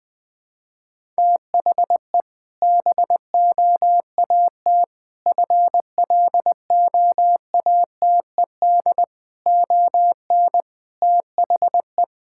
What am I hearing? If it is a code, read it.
Morse code, heard as a keyed tone, about 20 wpm, 700 Hz. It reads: THE BOAT FLOATED ON THE
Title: Sentences with 4-Word Phrases - 20wpm